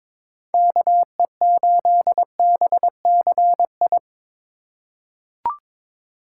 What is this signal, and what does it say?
Morse code, heard as a keyed tone, about 22 wpm, 700 Hz. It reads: KE8BCI
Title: USA Call Signs - 6 Characters - 4x Character Spacing - 22wpm